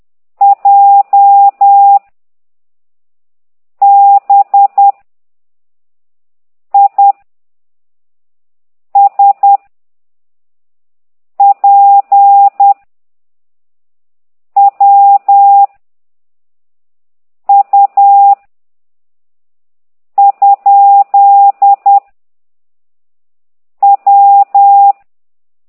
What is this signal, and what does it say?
Morse code, heard as a keyed tone, about 10 wpm, 800 Hz. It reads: J B I S P W U ? W
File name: ct2idw / exerc9